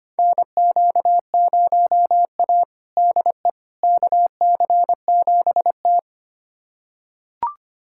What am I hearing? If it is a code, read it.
Morse code, heard as a keyed tone, about 25 wpm, 700 Hz. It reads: NQ0A DE KC7T
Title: USA Callsign DE Callsign - 4 Characters - 25wpm